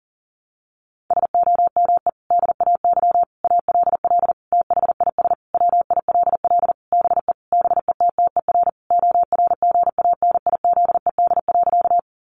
Morse code, heard as a keyed tone, about 37 wpm, 700 Hz. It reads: SOME DAY ALL THIS WILL BE BETTER ORGANIZED.
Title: Sentences from Top 1300 Words 40wpm